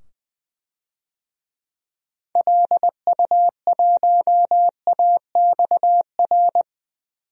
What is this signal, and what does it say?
Morse code, heard as a keyed tone, about 20 wpm, 700 Hz. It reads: LU1AXR
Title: DXCC-100 Call Signs - 6 Characters - 20wpm